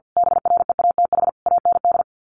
Morse code, heard as a keyed tone, about 50 wpm, 700 Hz. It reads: BREATH AND
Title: Sets of 5 Words Encore 50wpm